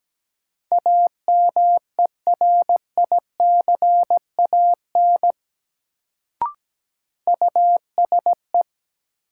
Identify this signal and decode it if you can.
Morse code, heard as a keyed tone, about 17 wpm, 700 Hz. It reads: AMERICAN USE
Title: Sets of 2 Words - New Word Drill - 17wpm